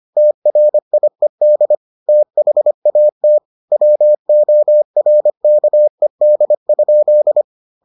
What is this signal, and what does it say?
Morse code, heard as a keyed tone, about 25 wpm, 600 Hz. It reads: TRIED THAT WORKED?